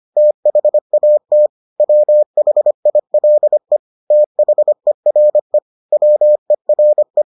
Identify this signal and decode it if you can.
Morse code, heard as a keyed tone, about 25 wpm, 600 Hz. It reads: THAT WHILE THERE WERE